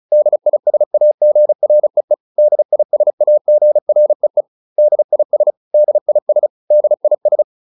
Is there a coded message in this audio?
Yes: Morse code from 0.1 to 7.4 s, heard as a keyed tone, about 34 wpm, 600 Hz. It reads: DISAGREE DISAGREE DIS DIS DIS